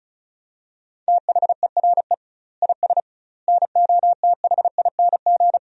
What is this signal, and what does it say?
Morse code, heard as a keyed tone, about 34 wpm, 700 Hz. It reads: THERE IS NOTHING
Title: 2x Word-Spacing - Sentences from Top 500 Words - 35wpm